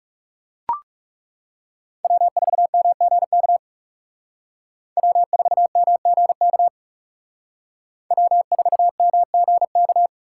Morse code, heard as a keyed tone, about 37 wpm, 700 Hz. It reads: W4MGK W4MGK W4MGK
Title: Speed Racing - USA 1x3 Call Signs 30wpm